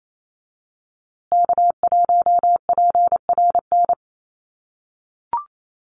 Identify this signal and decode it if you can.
Morse code, heard as a keyed tone, about 28 wpm, 700 Hz. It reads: K1PRN
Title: USA Call Signs Repeated 2x 28wpm